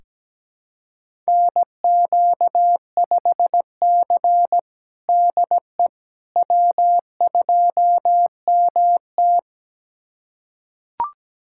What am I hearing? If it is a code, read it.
Morse code, heard as a keyed tone, about 17 wpm, 700 Hz. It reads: NQ5C DE W2MT
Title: USA Callsign DE Callsign - 4 Characters - 17wpm